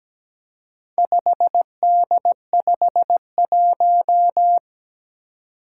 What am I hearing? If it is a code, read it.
Morse code, heard as a keyed tone, about 17 wpm, 700 Hz. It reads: 5D51
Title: Four Challenging Letter-Number Combinations 17wpm